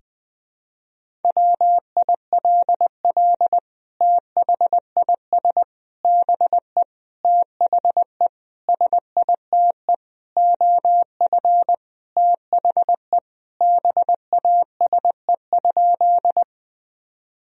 Morse code, heard as a keyed tone, about 20 wpm, 700 Hz. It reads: WILL THIS BE THE SITE OF THE BASE?
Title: Sentences from Top 1000 Words - Review Part 1 - 20wpm